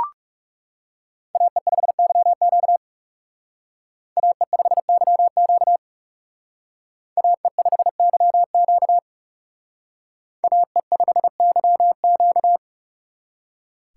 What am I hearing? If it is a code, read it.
Morse code, heard as a keyed tone, about 36 wpm, 700 Hz. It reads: AE5YQ AE5YQ AE5YQ AE5YQ
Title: Speed Racing - USA 2x2 Call Signs 30wpm